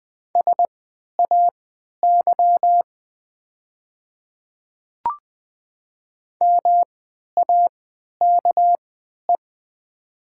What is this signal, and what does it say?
Morse code, heard as a keyed tone, about 20 wpm, spaced as for 11 wpm, 690 Hz. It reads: SAY MAKE
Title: 3x Character Spacing - Top 200 Words - 20wpm